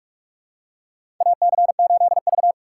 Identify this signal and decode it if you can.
Morse code, heard as a keyed tone, about 41 wpm, 700 Hz. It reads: AC8V
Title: USA Call Signs - 4 Characters - 4x Character Spacing - 45wpm